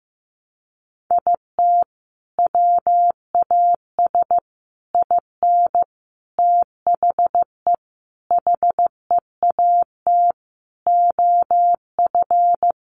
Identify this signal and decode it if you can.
Morse code, heard as a keyed tone, about 15 wpm, 700 Hz. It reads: IT WAS IN THE HEAT OF
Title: Sentences from Top 1300 Words 15wpm